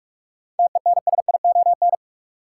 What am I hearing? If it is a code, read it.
Morse code, heard as a keyed tone, about 46 wpm, 700 Hz. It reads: TENSION
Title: Sentences from Top 1600 Words - New Word Drill Alt - 45wpm